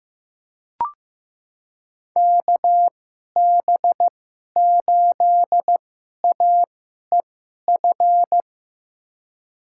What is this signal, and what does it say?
Morse code, heard as a keyed tone, about 15 wpm, spaced as for 11 wpm, 700 Hz. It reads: KB8AEF
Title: USA Call Signs - 6 Characters - 2x Character Spacing - 15wpm